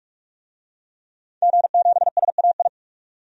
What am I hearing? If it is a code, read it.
Morse code, heard as a keyed tone, about 42 wpm, 700 Hz. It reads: G7SAI
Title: DXCC-100 Call Signs - 5 Characters - 45wpm